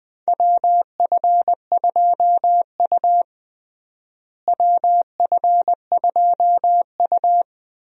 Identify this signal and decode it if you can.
Morse code, heard as a keyed tone, about 20 wpm, 690 Hz. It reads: WF2U WF2U
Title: USA Call Signs - 4 Characters - Repeated 3x - 20wpm